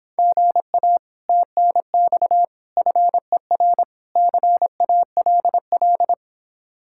Morse code, heard as a keyed tone, about 26 wpm, 700 Hz. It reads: GA TNX FER CALL